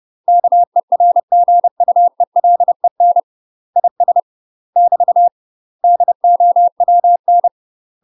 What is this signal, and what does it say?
Morse code, heard as a keyed tone, about 30 wpm, 700 Hz. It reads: KERGUELEN IS <BT> DOWN
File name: UK MASTER QSOs TRAINING / 30WPM / UK MASTER QSOs 0019